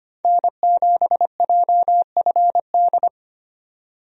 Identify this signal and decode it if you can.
Morse code, heard as a keyed tone, about 25 wpm, 700 Hz. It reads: N7JFD
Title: USA Call Signs - 5 Characters - 25wpm